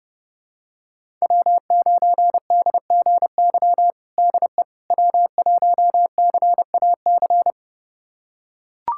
Morse code, heard as a keyed tone, about 30 wpm, 700 Hz. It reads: W9DGY DE W1CAC
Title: USA Callsign DE Callsign - 5 Characters - 30wpm